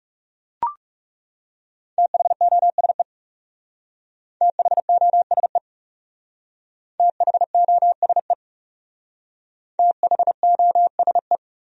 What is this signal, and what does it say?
Morse code, heard as a keyed tone, about 36 wpm, 700 Hz. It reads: THOSE THOSE THOSE THOSE
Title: Speed-Racing Top 300 Words 30wpm